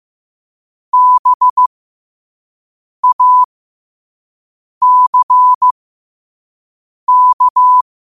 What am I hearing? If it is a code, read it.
Morse code, heard as a keyed tone, about 15 wpm, spaced as for 5 wpm, 1000 Hz. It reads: BACK
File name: qotdfiles / 5wpm